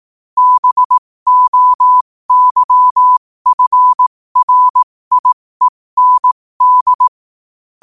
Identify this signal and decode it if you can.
Morse code, heard as a keyed tone, about 18 wpm, spaced as for 15 wpm, 1000 Hz. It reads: BOYFRIEND
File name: qotdfiles / 15wpm